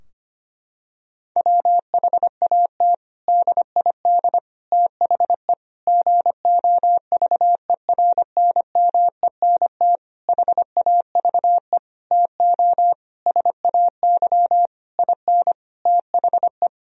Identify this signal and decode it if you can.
Morse code, heard as a keyed tone, about 25 wpm, 700 Hz. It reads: WHAT DID THE GOVERNMENT HAVE TO SAY IN THE
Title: Sentences from Top 1000 Words - Review Part 3 - 25wpm